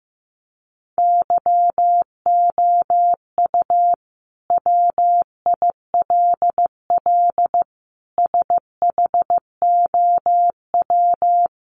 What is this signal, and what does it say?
Morse code, heard as a keyed tone, about 15 wpm, 700 Hz. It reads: YOU WILL SHOW
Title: Sentences from Top 600 Words 15wpm